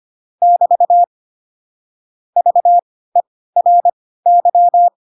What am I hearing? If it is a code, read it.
Morse code, heard as a keyed tone, about 25 wpm, spaced as for 16 wpm, 700 Hz. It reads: <BT> VERY
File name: Pinocchio / Pinocchio ch4 10 25 ik1hge